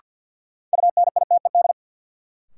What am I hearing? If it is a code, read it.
Morse code, heard as a keyed tone, about 47 wpm, 700 Hz. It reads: UNITED
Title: Sets of 2 Words - New Word Drill - 50wpm